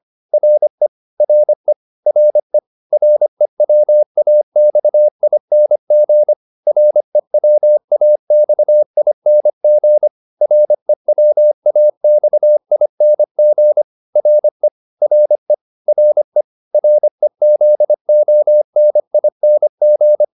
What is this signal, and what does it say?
Morse code, heard as a keyed tone, about 25 wpm, 600 Hz. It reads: RE RE RE REWAXING REWAXING REWAXING RE RE RE REZONING